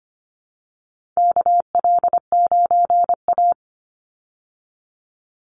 Morse code, heard as a keyed tone, about 25 wpm, 700 Hz. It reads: KL9A
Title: Call Signs Repeated 3x - 25wpm